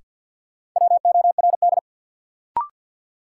Morse code, heard as a keyed tone, about 52 wpm, 700 Hz. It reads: WORD
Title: Sentences from Top 300 Words - New Word Drill - 50wpm